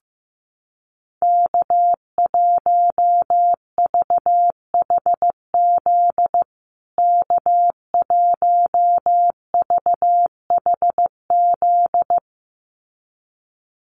Morse code, heard as a keyed tone, about 15 wpm, 700 Hz. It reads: K1VHZ K1VHZ
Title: USA Call Signs Repeated 2x 15wpm